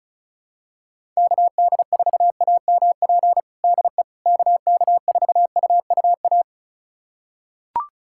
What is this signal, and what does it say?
Morse code, heard as a keyed tone, about 36 wpm, 700 Hz. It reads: KD4AMP DE KK4UUA
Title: USA Callsign DE Callsign 35wpm